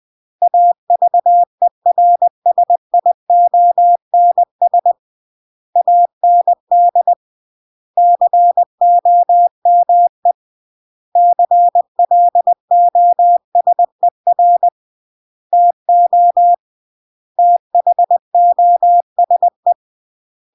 Morse code, heard as a keyed tone, about 20 wpm, 700 Hz. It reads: AVERSIONS AND COME CLOSER TO THOSE